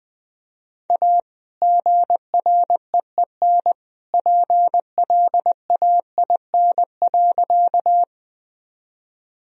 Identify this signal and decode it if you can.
Morse code, heard as a keyed tone, about 20 wpm, 700 Hz. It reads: A GREEN PLAIN.
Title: Sentences from Top 1000 Words - Review Part 2 - 20wpm